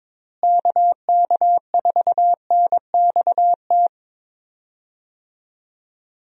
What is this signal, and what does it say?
Morse code, heard as a keyed tone, about 22 wpm, 700 Hz. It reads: KK4NXT
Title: USA Call Signs - 6 Characters - Repeated 2x - 22wpm